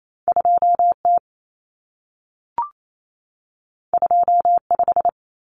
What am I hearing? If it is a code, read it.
Morse code, heard as a keyed tone, about 28 wpm, 700 Hz. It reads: 2T 25
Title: Common Cut Numbers - 2-Digit 28wpm